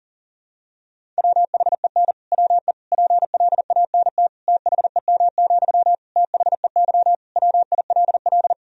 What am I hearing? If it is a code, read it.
Morse code, heard as a keyed tone, about 40 wpm, 700 Hz. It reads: WHEN WE PLANT THEM, THEY WILL
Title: Sentences from Top 1000 Words - Review Part 1 - 40wpm